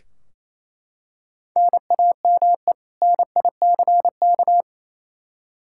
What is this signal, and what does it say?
Morse code, heard as a keyed tone, about 28 wpm, 700 Hz. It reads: NAME NICK
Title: Practice QSO - Exchange 1 - Beginning - 28wpm